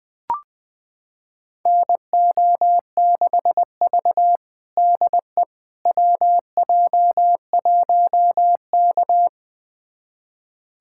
Morse code, heard as a keyed tone, about 21 wpm, 700 Hz. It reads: NO6V DE WJ1K
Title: USA Callsign DE Callsign - 4 Characters - 20wpm